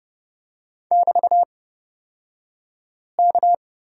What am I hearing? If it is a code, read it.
Morse code, heard as a keyed tone, about 30 wpm, spaced as for 5 wpm, 700 Hz. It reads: <BT>K
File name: Lesson 38d - Letter q - Two Characters 30wpm